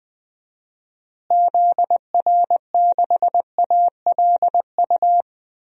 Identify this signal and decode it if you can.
Morse code, heard as a keyed tone, about 20 wpm, 700 Hz. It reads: ZR6ALU
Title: DXCC-100 Call Signs - 6 Characters - Repeated 3x - 20wpm